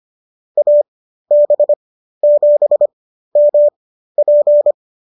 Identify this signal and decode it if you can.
Morse code, heard as a keyed tone, about 25 wpm, spaced as for 13 wpm, 590 Hz. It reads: AB7MP